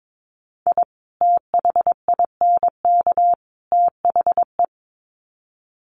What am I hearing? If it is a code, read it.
Morse code, heard as a keyed tone, about 22 wpm, 700 Hz. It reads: I THINK THE
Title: Sets of 3 Words 22wpm